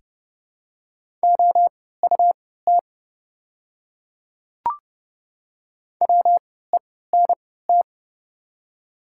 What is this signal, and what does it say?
Morse code, heard as a keyed tone, about 30 wpm, spaced as for 17 wpm, 700 Hz. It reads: OUT WENT
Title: 3x Character Spacing - Top 200 Words - 30wpm